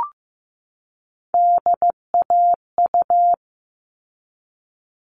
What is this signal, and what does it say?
Morse code, heard as a keyed tone, about 15 wpm, 700 Hz. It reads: DAU
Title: All Cut Numbers - 3-Digit 15wpm